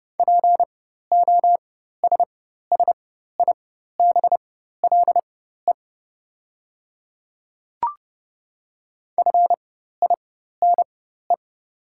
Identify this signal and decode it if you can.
Morse code, heard as a keyed tone, about 30 wpm, spaced as for 14 wpm, 690 Hz. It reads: POSSIBLE FINE
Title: 4x Character Spacing - Top 500 Words - 30wpm